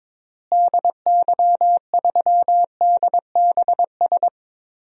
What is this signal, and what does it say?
Morse code, heard as a keyed tone, about 21 wpm, 700 Hz. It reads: DY3DBS
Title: DXCC-100 Call Signs - 6 Characters - Repeated 2x - 22wpm